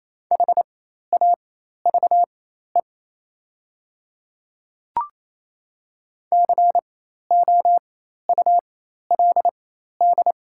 Morse code, heard as a keyed tone, about 28 wpm, spaced as for 13 wpm, 700 Hz. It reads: HAVE COULD
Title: 4x Character Spacing - Top 100 Words - 28wpm